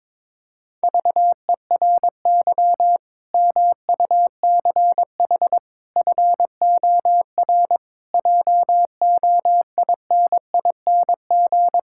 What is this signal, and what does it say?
Morse code, heard as a keyed tone, about 22 wpm, 700 Hz. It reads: VERY MUCH FOR JOINING
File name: practice / 22wpm / Build A Phrase - 5-Word Phrases - 22wpm